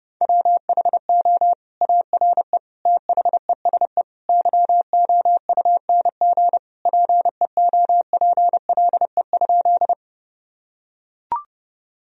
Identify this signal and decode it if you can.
Morse code, heard as a keyed tone, about 30 wpm, 700 Hz. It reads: WHO ARE THESE YOUNG PEOPLE?
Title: Sentences from Top 1000 Words - Review Part 1 - 30wpm